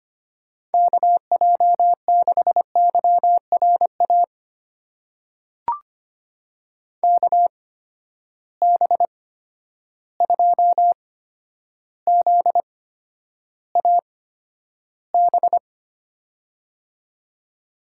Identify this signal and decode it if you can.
Morse code, heard as a keyed tone, about 25 wpm, spaced as for 11 wpm, 700 Hz. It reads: KJ6YRA KB2ZAB
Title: USA Call Signs - 6 Characters - 8x Character Spacing - 25wpm